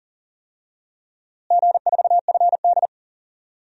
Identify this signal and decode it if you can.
Morse code, heard as a keyed tone, about 39 wpm, 700 Hz. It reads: G4FD
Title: DXCC-100 Call Signs - 4 Characters - Repeated 3x - 40wpm